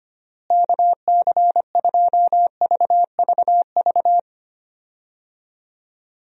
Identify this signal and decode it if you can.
Morse code, heard as a keyed tone, about 25 wpm, 700 Hz. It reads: KC2VVV
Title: USA Call Signs - 6 Characters - Repeated 2x - 25wpm